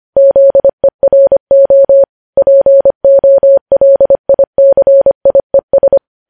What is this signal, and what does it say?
Morse code, heard as a keyed tone, about 25 wpm, 550 Hz. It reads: ZERO POLICIES
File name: podcast / News headlines-1x-25x25